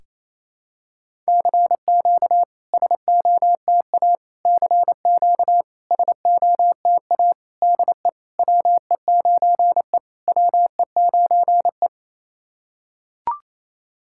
Morse code, heard as a keyed tone, about 28 wpm, 700 Hz. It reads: CQ SOTA CQ SOTA DE WE9E WE9E
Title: SOTA QSO Practice 28wpm